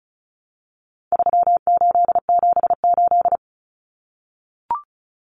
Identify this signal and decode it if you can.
Morse code, heard as a keyed tone, about 34 wpm, 700 Hz. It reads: 3878
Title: Numbers - 4-Digit 35wpm